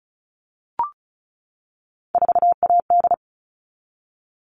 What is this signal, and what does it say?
Morse code, heard as a keyed tone, about 34 wpm, 700 Hz. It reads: 4AD
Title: All Cut Numbers - 3-Digit 35wpm